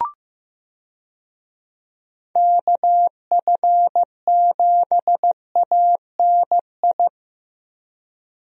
Morse code, heard as a keyed tone, about 15 wpm, 700 Hz. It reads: KF7ANI
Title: USA Call Signs - 6 Characters - Repeated 2x - 15wpm